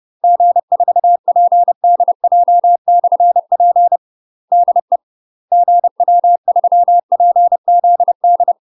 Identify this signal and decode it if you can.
Morse code, heard as a keyed tone, about 30 wpm, 700 Hz. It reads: G4PDJ/P DE GW3PZD
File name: UK MASTER QSOs TRAINING / 30WPM / UK MASTER QSOs 0026